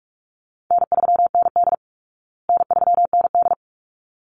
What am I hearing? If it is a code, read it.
Morse code, heard as a keyed tone, about 44 wpm, 700 Hz. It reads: N3ND N3ND
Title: Easy Call Signs Repeated 2x - 45wpm